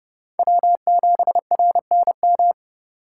Morse code, heard as a keyed tone, about 30 wpm, 700 Hz. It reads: W7RNM
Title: USA Call Signs - 5 Characters - 2x Character Spacing - 30wpm